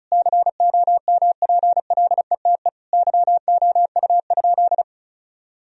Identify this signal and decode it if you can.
Morse code, heard as a keyed tone, about 36 wpm, 700 Hz. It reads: COMPLETE YOU?
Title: Sentences from Top 1000 Words - Review Part 2 - 35wpm